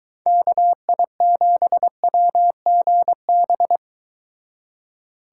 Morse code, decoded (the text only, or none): KI7WGB